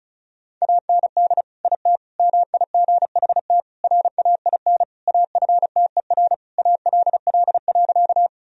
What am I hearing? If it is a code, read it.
Morse code, heard as a keyed tone, about 36 wpm, 700 Hz. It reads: AND IT MIGHT RAIN AFTER ALL.
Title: Sentences from Top 1000 Words - Review Part 1 - 35wpm